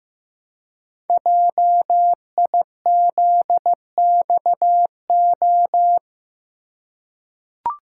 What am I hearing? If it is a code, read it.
Morse code, heard as a keyed tone, about 15 wpm, 700 Hz. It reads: JIZXO
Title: Code Groups - 5 Letters - 15wpm